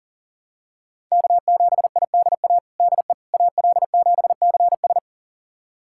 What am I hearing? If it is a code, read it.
Morse code, heard as a keyed tone, about 40 wpm, 700 Hz. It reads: K7IDA DE AL7CS
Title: USA Callsign DE Callsign - 5 Characters - 40wpm